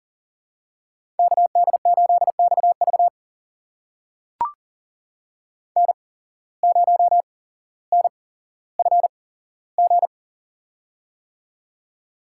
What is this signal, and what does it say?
Morse code, heard as a keyed tone, about 40 wpm, spaced as for 17 wpm, 700 Hz. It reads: KD8XV N0NFG
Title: USA Call Signs - 5 Characters - 8x Character Spacing - 40wpm